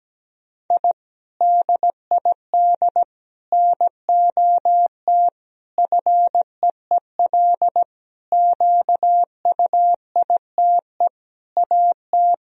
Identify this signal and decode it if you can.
Morse code, heard as a keyed tone, about 17 wpm, 700 Hz. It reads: I DID NOT FEEL QUITE AT
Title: Sentences from Top 1000 Words - Review Part 1 - 17wpm